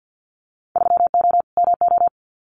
Morse code, heard as a keyed tone, about 50 wpm, 700 Hz. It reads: 3O MO
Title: Winter Field Day Exchanges Encore 50wpm